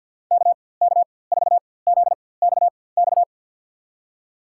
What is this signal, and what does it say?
Morse code, heard as a keyed tone, about 54 wpm, spaced as for 24 wpm, 700 Hz. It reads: KK4ZXX
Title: USA Call Signs - 6 Characters - 4x Character Spacing - 50wpm